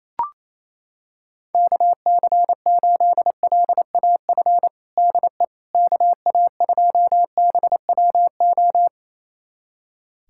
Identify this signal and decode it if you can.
Morse code, heard as a keyed tone, about 28 wpm, 700 Hz. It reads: KC8LAF DE KA2BWO
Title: USA Callsign DE Callsign - 6 Characters - 28wpm